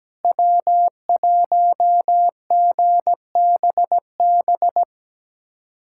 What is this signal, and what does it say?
Morse code, heard as a keyed tone, about 17 wpm, 700 Hz. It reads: W1GBB
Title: USA Callsign DE Callsign 17wpm